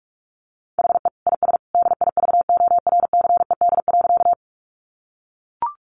Morse code, heard as a keyed tone, about 44 wpm, 700 Hz. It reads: HE IS DIVORCED.
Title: Additional Easy Phrases 45wpm